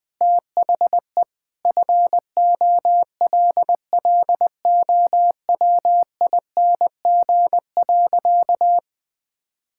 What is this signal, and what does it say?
Morse code, heard as a keyed tone, about 21 wpm, 700 Hz. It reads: THE FOLLOWING.
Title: Sentences with 9-Word Phrases - 20wpm